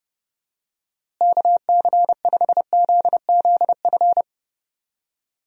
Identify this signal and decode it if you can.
Morse code, heard as a keyed tone, about 32 wpm, 700 Hz. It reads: KC5ZZF
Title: USA Call Signs - 6 Characters - Repeated 3x - 30wpm